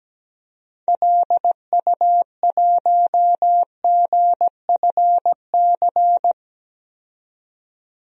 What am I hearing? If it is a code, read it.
Morse code, heard as a keyed tone, about 17 wpm, 700 Hz. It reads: LU1GFC
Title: DXCC-100 Call Signs - 6 Characters - 17wpm